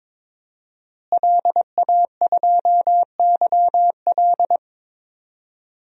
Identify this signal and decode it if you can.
Morse code, heard as a keyed tone, about 22 wpm, 700 Hz. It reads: LA2YL
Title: DXCC-100 Call Signs - 5 Characters - Repeated 2x - 22wpm